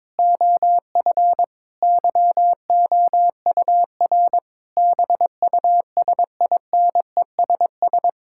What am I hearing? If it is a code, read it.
Morse code, heard as a keyed tone, about 22 wpm, 700 Hz. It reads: OF YOUR BUSINESS
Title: Sentences from Top 1000 Words - Review Part 6 - 22wpm